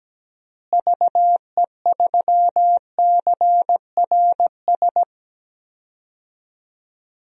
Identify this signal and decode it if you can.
Morse code, heard as a keyed tone, about 17 wpm, 700 Hz. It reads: VE3CRS
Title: DXCC-100 Call Signs - 6 Characters - Repeated 2x - 17wpm